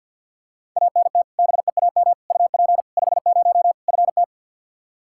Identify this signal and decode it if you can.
Morse code, heard as a keyed tone, about 52 wpm, 700 Hz. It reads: ANT BEAM UP 50 FT
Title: Practice QSO - Exchange 2 - Info - 50wpm